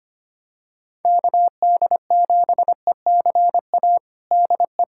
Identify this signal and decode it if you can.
Morse code, heard as a keyed tone, about 25 wpm, 700 Hz. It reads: KD7ECA DE
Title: USA Callsign DE Callsign - 6 Characters - 25wpm